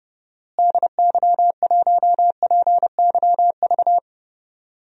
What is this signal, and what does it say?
Morse code, heard as a keyed tone, about 31 wpm, 700 Hz. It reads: DY1PYV
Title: DXCC-100 Call Signs - 6 Characters - Repeated 3x - 30wpm